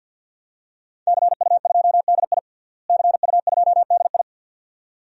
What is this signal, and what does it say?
Morse code, heard as a keyed tone, about 47 wpm, 700 Hz. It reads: CU2DI CU2DI
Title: DXCC-100 Call Signs - 5 Characters - Repeated 2x - 50wpm